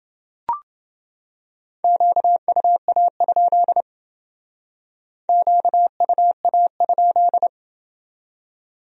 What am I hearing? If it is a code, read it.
Morse code, heard as a keyed tone, about 29 wpm, 700 Hz. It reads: QUA? QUA?